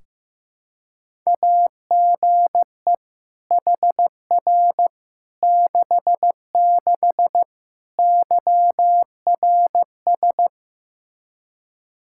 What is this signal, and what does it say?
Morse code, heard as a keyed tone, about 16 wpm, 700 Hz. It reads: AGE HR 66 YRS
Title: Practice QSO - Exchange 3 - More Info - 15wpm